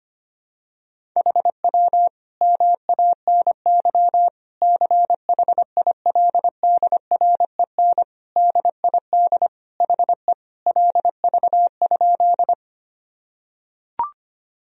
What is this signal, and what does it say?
Morse code, heard as a keyed tone, about 25 wpm, 700 Hz. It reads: HW MANY CHILDREN DID HE LV?
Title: Sentences from Top 400 Words and Abbreviations 25wpm